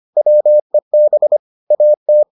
Morse code, heard as a keyed tone, about 25 wpm, 600 Hz. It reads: WEB AT